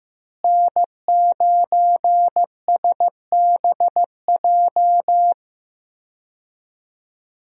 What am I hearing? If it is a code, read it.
Morse code, heard as a keyed tone, about 15 wpm, 700 Hz. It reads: N9SBJ
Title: USA Call Signs - 5 Characters - Repeated 3x - 15wpm